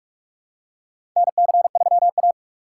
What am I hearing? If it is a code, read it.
Morse code, heard as a keyed tone, about 46 wpm, 700 Hz. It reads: NC3A
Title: USA Call Signs - 4 Characters - 6x Character Spacing - 45wpm